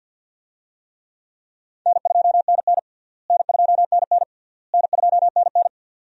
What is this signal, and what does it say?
Morse code, heard as a keyed tone, about 52 wpm, 700 Hz. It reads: N2NN N2NN N2NN
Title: USA Call Signs - 4 Characters - Repeated 3x - 50wpm